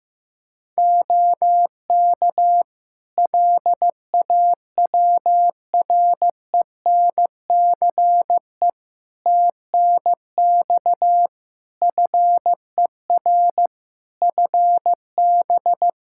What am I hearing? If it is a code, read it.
Morse code, heard as a keyed tone, about 16 wpm, 700 Hz. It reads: OK LAWRENCE TNX FER FB
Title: Practice QSO - Exchange 4 - Ending - 15wpm